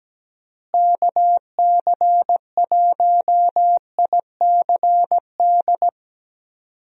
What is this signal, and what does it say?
Morse code, heard as a keyed tone, about 17 wpm, 700 Hz. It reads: KC1ICD
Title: USA Call Signs - 6 Characters - Repeated 2x - 17wpm